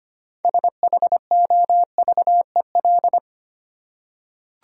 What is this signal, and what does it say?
Morse code, heard as a keyed tone, about 25 wpm, 700 Hz. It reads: SHOVEL
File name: practice / 25wpm / Easy Words with Hints - 25wpm